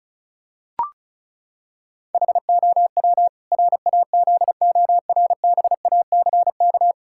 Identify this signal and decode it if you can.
Morse code, heard as a keyed tone, about 36 wpm, 700 Hz. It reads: HOW RAZORBACK
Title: Pangram Challenge v2 - 35wpm